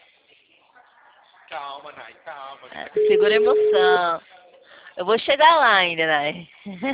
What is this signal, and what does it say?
Ringback tone (Europe, 425 Hz), with speech.